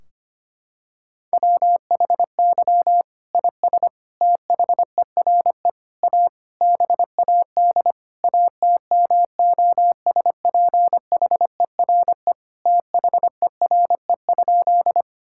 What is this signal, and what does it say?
Morse code, heard as a keyed tone, about 25 wpm, 700 Hz. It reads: WHY IS THERE A BAD ATMOSPHERE THERE?
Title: Sentences from Top 1600 Words 25wpm